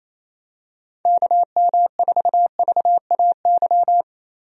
Morse code, heard as a keyed tone, about 27 wpm, 700 Hz. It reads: KM4VAY
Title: USA Call Signs - 6 Characters - Repeated 2x - 28wpm